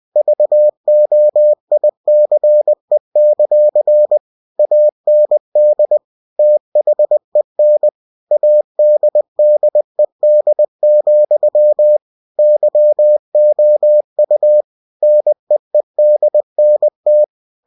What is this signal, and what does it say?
Morse code, heard as a keyed tone, about 20 wpm, 600 Hz. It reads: VOICE; AND THEN ADDED, YOU NEEDNT
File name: cw-books / winnie-the-pooh / winnie-20wpm-page-0013